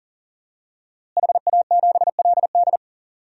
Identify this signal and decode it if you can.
Morse code, heard as a keyed tone, about 39 wpm, 700 Hz. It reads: HA7LD
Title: DXCC-100 Call Signs - 5 Characters - Repeated 3x - 40wpm